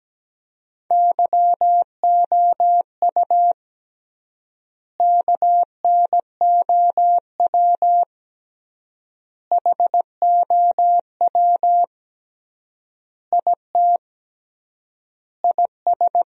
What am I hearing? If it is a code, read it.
Morse code, heard as a keyed tone, about 17 wpm, 700 Hz. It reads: YOU KNOW HOW IT IS
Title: 3x Word-Spacing - Sentences from Top 100 Words - 17wpm